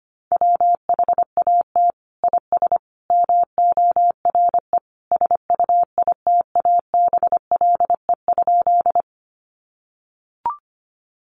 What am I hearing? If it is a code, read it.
Morse code, heard as a keyed tone, about 25 wpm, 700 Hz. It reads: WHAT IS MORE SUITABLE?
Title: Sentences from Top 1400 Words 25wpm